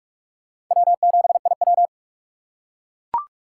Morse code, heard as a keyed tone, about 42 wpm, 700 Hz. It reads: W7IW
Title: USA Call Signs - 4 Characters - 2x Character Spacing - 45wpm